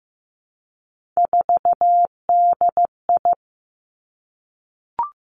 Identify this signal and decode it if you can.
Morse code, heard as a keyed tone, about 15 wpm, 700 Hz. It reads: 4DI
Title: Three Challenging Letter-Number Combinations 15wpm